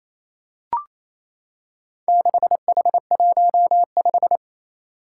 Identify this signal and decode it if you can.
Morse code, heard as a keyed tone, about 28 wpm, 700 Hz. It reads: 6H15
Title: Four Challenging Letter-Number Combinations 28wpm